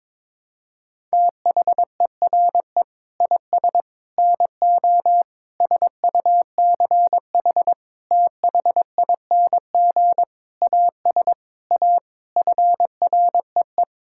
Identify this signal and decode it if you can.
Morse code, heard as a keyed tone, about 22 wpm, 700 Hz. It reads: THERE IS NO SUCH THING AS A FREE
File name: Sets of 9 Words - 22wpm